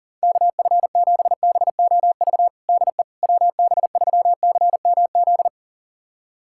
Morse code, heard as a keyed tone, about 40 wpm, 700 Hz. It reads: KF7BOV DE WB3CMZ
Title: USA Callsign DE Callsign 40wpm